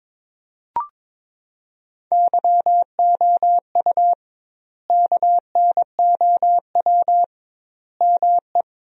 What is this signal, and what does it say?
Morse code, heard as a keyed tone, about 22 wpm, 700 Hz. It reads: YOU KNOW ME
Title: 2x Word-Spacing - Sentences from Top 400 Words - 22wpm